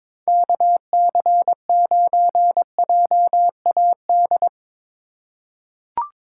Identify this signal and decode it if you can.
Morse code, heard as a keyed tone, about 21 wpm, 700 Hz. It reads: KC9JAD
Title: USA Call Signs - 2x Character Spacing - 22wpm